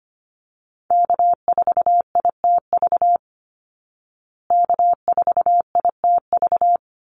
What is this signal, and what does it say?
Morse code, heard as a keyed tone, about 25 wpm, 700 Hz. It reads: K4ITV K4ITV
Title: Call Signs Repeated 3x - 25wpm